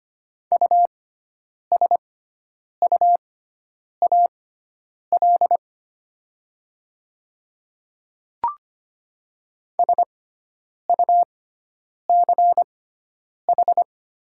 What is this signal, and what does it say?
Morse code, heard as a keyed tone, about 25 wpm, spaced as for 9 wpm, 700 Hz. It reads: USUAL SUCH
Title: 6x Character Spacing - Top 500 Words - 25wpm